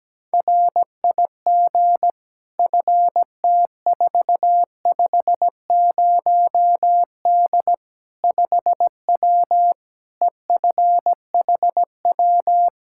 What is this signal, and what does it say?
Morse code, heard as a keyed tone, about 17 wpm, 700 Hz. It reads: RIG FT450D 5W EFHW
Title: Practice QSO - Exchange 2 - Info - 17wpm